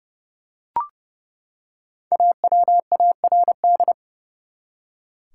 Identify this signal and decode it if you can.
Morse code, heard as a keyed tone, about 30 wpm, 700 Hz. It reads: AWARD
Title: Five Letter Words - 30wpm